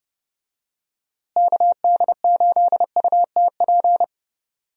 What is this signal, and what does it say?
Morse code, heard as a keyed tone, about 30 wpm, 700 Hz. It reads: KD8UTP
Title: USA Call Signs - 6 Characters - Repeated 2x - 30wpm